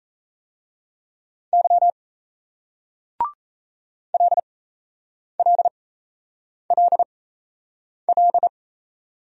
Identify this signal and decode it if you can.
Morse code, heard as a keyed tone, about 38 wpm, spaced as for 8 wpm, 700 Hz. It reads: Y LLLL